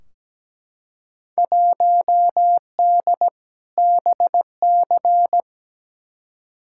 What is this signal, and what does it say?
Morse code, heard as a keyed tone, about 17 wpm, 700 Hz. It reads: 1D BC
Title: Field Day Exchanges - 17wpm